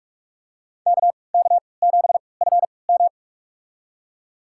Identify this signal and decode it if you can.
Morse code, heard as a keyed tone, about 46 wpm, spaced as for 25 wpm, 700 Hz. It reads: KK7FM
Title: USA Call Signs - 5 Characters - 3x Character Spacing - 45wpm